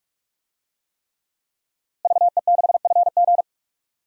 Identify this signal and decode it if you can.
Morse code, heard as a keyed tone, about 43 wpm, 700 Hz. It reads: VE6FG
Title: DXCC-100 Call Signs - 5 Characters - Repeated 3x - 45wpm